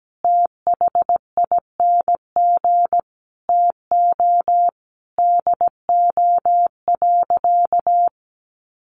Morse code, heard as a keyed tone, about 17 wpm, 700 Hz. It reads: THING TO DO.
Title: Sentences from Top 100 Words 17wpm